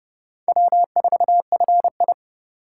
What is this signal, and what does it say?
Morse code, heard as a keyed tone, about 30 wpm, 700 Hz. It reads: W4FI